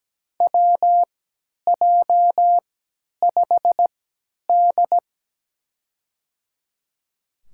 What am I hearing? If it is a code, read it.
Morse code, heard as a keyed tone, about 17 wpm, spaced as for 10 wpm, 700 Hz. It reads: WJ5D
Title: USA Call Signs - 4 Characters - 3x Character Spacing - 17wpm